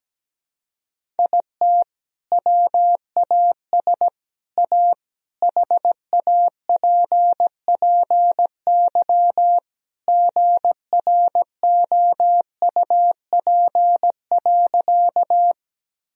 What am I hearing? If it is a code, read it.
Morse code, heard as a keyed tone, about 17 wpm, 700 Hz. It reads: IT WAS A HAPPY GROUP.
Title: Sentences from Top 1000 Words - Review Part 1 - 17wpm